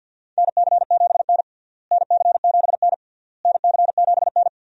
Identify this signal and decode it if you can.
Morse code, heard as a keyed tone, about 52 wpm, 700 Hz. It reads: NC7N NC7N NC7N
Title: USA Call Signs - 4 Characters - Repeated 3x - 50wpm